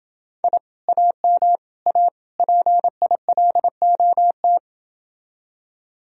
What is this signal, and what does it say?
Morse code, heard as a keyed tone, about 29 wpm, 700 Hz. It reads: I AM A PILOT